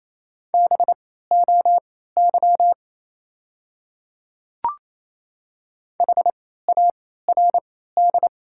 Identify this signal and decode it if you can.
Morse code, heard as a keyed tone, about 28 wpm, spaced as for 16 wpm, 690 Hz. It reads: BOY HARD
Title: 3x Character Spacing - Top 300 Words - 28wpm